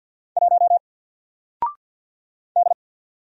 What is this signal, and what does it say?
Morse code, heard as a keyed tone, about 48 wpm, 700 Hz. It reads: 1 D
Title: Single Letter-Number Alt 50wpm